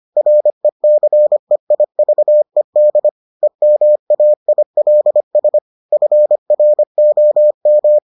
Morse code, heard as a keyed tone, about 25 wpm, 600 Hz. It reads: RECEIVED EMAILS FROM